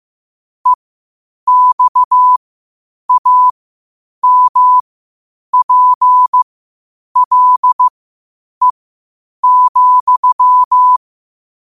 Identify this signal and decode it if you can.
Morse code, heard as a keyed tone, about 15 wpm, spaced as for 8 wpm, 1000 Hz. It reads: EXAMPLE,
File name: qotdfiles / 8wpm